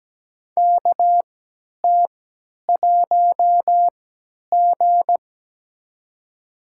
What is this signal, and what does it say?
Morse code, heard as a keyed tone, about 17 wpm, spaced as for 10 wpm, 700 Hz. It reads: KT1G
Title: USA Call Signs - 4 Characters - 3x Character Spacing - 17wpm